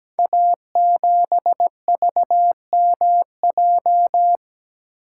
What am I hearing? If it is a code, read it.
Morse code, heard as a keyed tone, about 17 wpm, 700 Hz. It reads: A7VMJ
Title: Code Groups - 5 Letter:Numbers - 17wpm